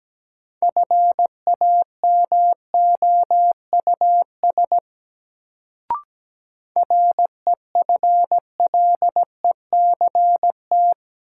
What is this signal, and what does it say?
Morse code, heard as a keyed tone, about 17 wpm, 700 Hz. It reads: FAMOUS REFLECT
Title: Sentences from Top 1500 Words - New Word Drill - 17wpm